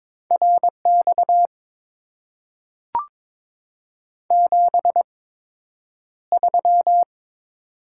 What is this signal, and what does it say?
Morse code, heard as a keyed tone, about 22 wpm, spaced as for 9 wpm, 700 Hz. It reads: RX 73
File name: Top 90 QSO Elements - 8x Character Spacing - 22wpm